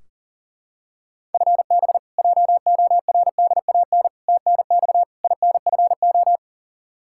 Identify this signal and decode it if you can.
Morse code, heard as a keyed tone, about 40 wpm, 700 Hz. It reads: FB JORDAN TNX INFO
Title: Practice QSO - Exchange 3 - More Info - 40wpm